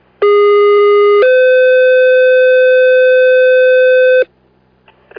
Two-tone page, heard as a one-second tone, then a three-second tone, about 410 then 510 Hz, with noise. It is heard from 0.2 to 4.2 s.